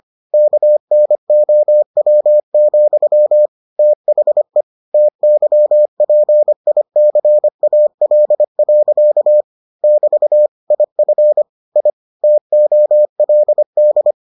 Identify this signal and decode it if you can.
Morse code, heard as a keyed tone, about 25 wpm, 600 Hz. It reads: KNOW, THE TYPICAL. <BT> IF I TOLD